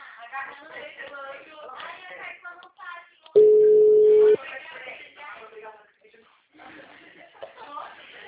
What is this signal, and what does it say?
Ringback tone (Europe, 425 Hz), with speech.
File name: out- -2001- - - .981